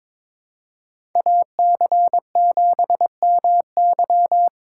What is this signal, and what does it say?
Morse code, heard as a keyed tone, about 22 wpm, 700 Hz. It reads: AC7MY
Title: USA Call Signs - 5 Characters - 3x Character Spacing - 22wpm